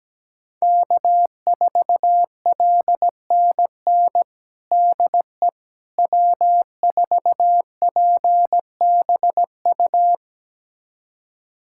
Morse code, heard as a keyed tone, about 17 wpm, 700 Hz. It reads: K4LNN DE W4PBU
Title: USA Callsign DE Callsign - 5 Characters - 17wpm